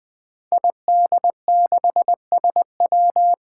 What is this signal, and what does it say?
Morse code, heard as a keyed tone, about 20 wpm, 700 Hz. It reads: ID6SW
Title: Five Challenging Letter-Number Combinations 20wpm